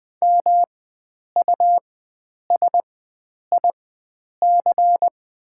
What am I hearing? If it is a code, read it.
Morse code, heard as a keyed tone, about 20 wpm, spaced as for 9 wpm, 700 Hz. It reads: MUSIC
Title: 4x Character Spacing - Top 500 Words - 20wpm